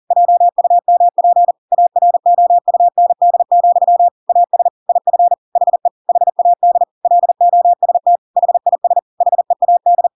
Morse code, heard as a keyed tone, about 40 wpm, 700 Hz. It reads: JUMP AROUND, AS IF HE HAD LOST HIS HEAD
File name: Pinocchio / Pinocchio ch8 40 40 ik1hge